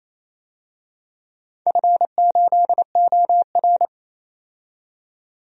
Morse code, heard as a keyed tone, about 29 wpm, 700 Hz. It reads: F8OR
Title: DXCC-100 Call Signs - 4 Characters - 28wpm